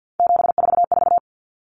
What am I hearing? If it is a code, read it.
Morse code, heard as a keyed tone, about 50 wpm, 700 Hz. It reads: 744
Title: Common Cut Numbers - 3-Digit 50wpm